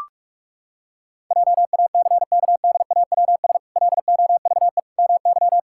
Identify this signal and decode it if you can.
Morse code, heard as a keyed tone, about 42 wpm, 700 Hz. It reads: JACKDAWS LOVE MY
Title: Pangram Challenge v2 - 45wpm